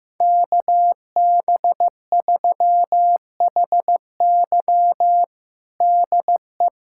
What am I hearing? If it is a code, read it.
Morse code, heard as a keyed tone, about 15 wpm, 700 Hz. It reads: KB3HY DE
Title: USA Callsign DE Callsign 15wpm